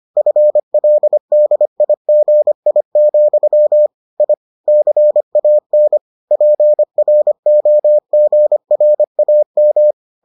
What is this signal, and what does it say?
Morse code, heard as a keyed tone, about 25 wpm, 600 Hz. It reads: FLDIGI, I CAN PROGRAM